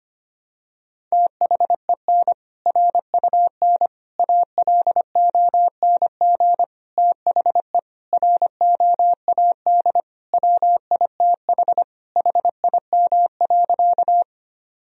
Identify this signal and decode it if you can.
Morse code, heard as a keyed tone, about 25 wpm, 700 Hz. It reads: THEN RUN ALONG THE ROAD WITH HIM.
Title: Sentences from Top 1000 Words - Review Part 2 - 25wpm